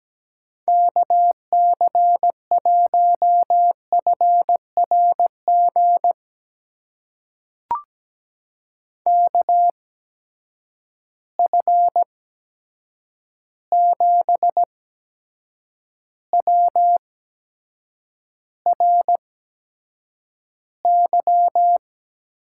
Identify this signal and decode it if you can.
Morse code, heard as a keyed tone, about 17 wpm, spaced as for 7 wpm, 700 Hz. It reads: KC1FRG KF7WRY
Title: USA Call Signs - 6 Characters - 8x Character Spacing - 17wpm